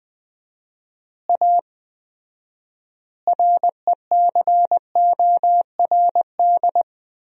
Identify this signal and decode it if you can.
Morse code, heard as a keyed tone, about 20 wpm, 700 Hz. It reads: A RECORD
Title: 4x Word-Spacing - Sentences from Top 500 Words - 20wpm